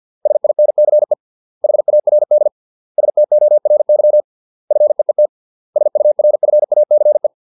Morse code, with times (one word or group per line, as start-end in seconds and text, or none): 0.3-1.1 SINCE
1.6-2.5 HARD
3.0-4.2 STORY
4.7-5.3 FEET
5.8-7.3 SURFACE